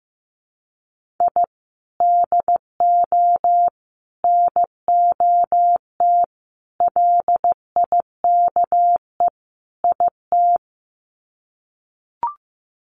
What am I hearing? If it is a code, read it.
Morse code, heard as a keyed tone, about 15 wpm, 700 Hz. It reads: I DO NOT LIKE IT
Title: Sets of 5 Words Encore 15wpm